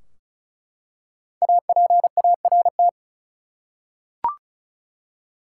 Morse code, heard as a keyed tone, about 34 wpm, 700 Hz. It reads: APART
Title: Five Letter Words - 35wpm